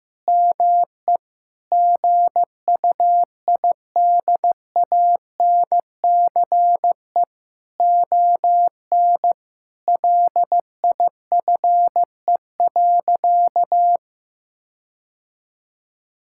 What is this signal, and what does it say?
Morse code, heard as a keyed tone, about 16 wpm, 700 Hz. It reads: ME GUIDANCE ON LIFE.
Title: Sentences from Top 1600 Words 15wpm